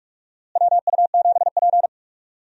Morse code, heard as a keyed tone, about 46 wpm, 700 Hz. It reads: WU7P
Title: USA Call Signs - 4 Characters - Repeated 2x - 45wpm